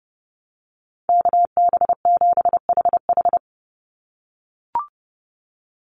Morse code, heard as a keyed tone, about 30 wpm, 700 Hz. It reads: KB7HH
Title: NA SOTA Chaser Callsigns 30wpm